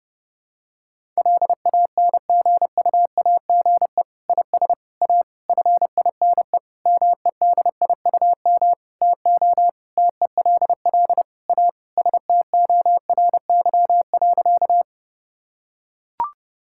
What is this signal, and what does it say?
Morse code, heard as a keyed tone, about 32 wpm, 700 Hz. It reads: LANGUAGE IS A FINE MEDIUM TO TELL A STORY.
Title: Sentences from Top 1000 Words - Review Part 6 - 30wpm